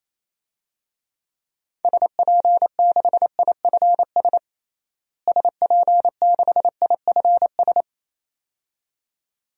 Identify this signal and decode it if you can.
Morse code, heard as a keyed tone, about 28 wpm, 700 Hz. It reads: SP6IFS SP6IFS
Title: DXCC-100 Call Signs - 6 Characters - Repeated 2x - 28wpm